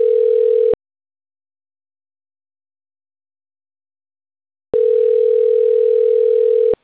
Ringback tone (North America), 440 + 480 Hz.